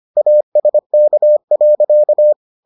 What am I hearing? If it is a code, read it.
Morse code, heard as a keyed tone, about 25 wpm, 600 Hz. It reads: ASK.